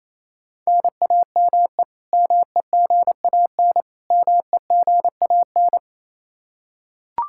Morse code, heard as a keyed tone, about 28 wpm, 700 Hz. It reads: NAME MEGAN MEGAN
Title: Practice QSO - Exchange 1 - Beginning - 28wpm